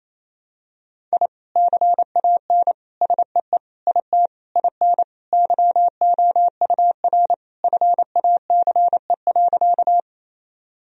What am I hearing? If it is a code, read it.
Morse code, heard as a keyed tone, about 28 wpm, 700 Hz. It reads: I CAN SEE IT IN YOUR FACE.
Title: Sentences from Top 1000 Words - Review Part 2 - 28wpm